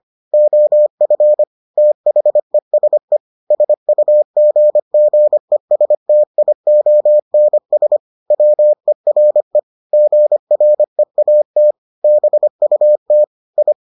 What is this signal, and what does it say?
Morse code, heard as a keyed tone, about 25 wpm, 600 Hz. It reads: OF THESE SUGGESTIONS WERE GREAT BUT I